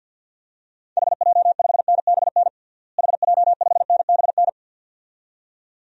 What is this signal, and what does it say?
Morse code, heard as a keyed tone, about 52 wpm, 700 Hz. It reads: HJ5NBN HJ5NBN
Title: DXCC-100 Call Signs - 6 Characters - Repeated 2x - 50wpm